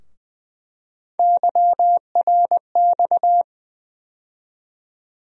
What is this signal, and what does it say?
Morse code, heard as a keyed tone, about 20 wpm, 700 Hz. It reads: YRX
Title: USA Call Sign Postfixes - 3 Letters 20wpm